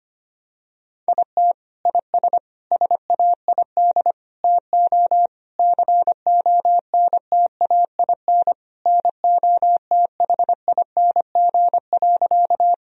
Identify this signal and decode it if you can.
Morse code, heard as a keyed tone, about 25 wpm, 700 Hz. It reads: IT IS SAID TO CONTAIN NOTHING.
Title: Sentences from Top 1000 Words - Review Part 2 - 25wpm